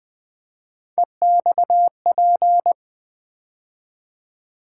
Morse code, heard as a keyed tone, about 20 wpm, 700 Hz. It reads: EXP
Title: Build A Word - 3 Letters At A Time - 8-Letter Words - 20wpm